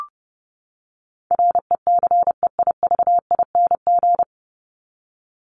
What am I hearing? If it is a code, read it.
Morse code, heard as a keyed tone, about 30 wpm, 700 Hz. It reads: RECEIVING
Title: ING Suffix - Top 100 Words with Base Words 30wpm